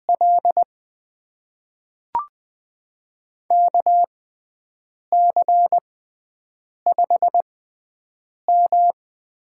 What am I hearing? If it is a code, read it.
Morse code, heard as a keyed tone, about 20 wpm, spaced as for 7 wpm, 700 Hz. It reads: L KC5M
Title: USA Call Signs - 6x Character Spacing - 20wpm